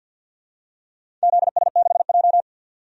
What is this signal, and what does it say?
Morse code, heard as a keyed tone, about 46 wpm, 700 Hz. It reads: ZS6J
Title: DXCC-100 Call Signs - 4 Characters - 50wpm